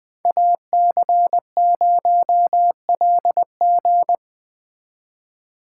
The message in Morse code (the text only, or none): AC0LG